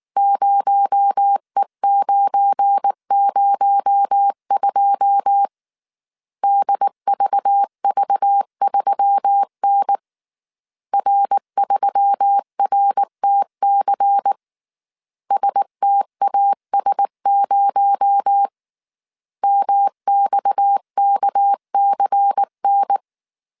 Morse code, heard as a keyed tone, about 19 wpm, 800 Hz. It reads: 0E902 DVV3N R3RTC STAS0 MXKCN
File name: CW / DET / 1 9 1500C S A 18wpm 2459